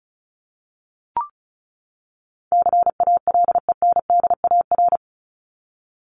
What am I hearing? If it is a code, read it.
Morse code, heard as a keyed tone, about 34 wpm, 700 Hz. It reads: CALENDAR
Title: Top 1501-1750 Words 35wpm